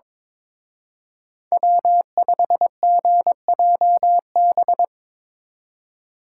Morse code, heard as a keyed tone, about 24 wpm, 700 Hz. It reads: W5GJB
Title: Speed Racing - USA 1x3 Call Signs 22wpm